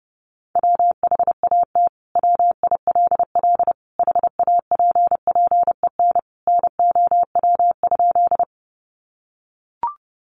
Morse code, heard as a keyed tone, about 30 wpm, 700 Hz. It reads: WHAT WILL HAPPEN NOW?
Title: Sentences from Top 400 Words 30wpm